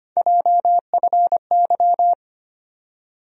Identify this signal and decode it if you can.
Morse code, heard as a keyed tone, about 25 wpm, 700 Hz. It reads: JFY
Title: USA Call Sign Postfixes - 3 Letters 25wpm